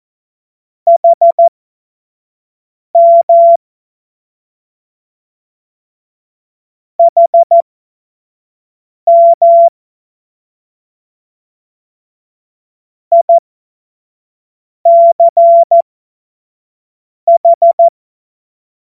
Morse code, heard as a keyed tone, about 14 wpm, spaced as for 5 wpm, 680 Hz. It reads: HM HM ICH
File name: Morsekurs / Witze / Witz 14. Februar 2023 25 F70